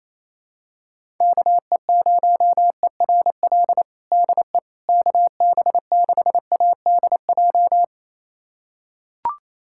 Morse code, heard as a keyed tone, about 28 wpm, 700 Hz. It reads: KE0ERL DE KB6ADJ
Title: USA Callsign DE Callsign 28wpm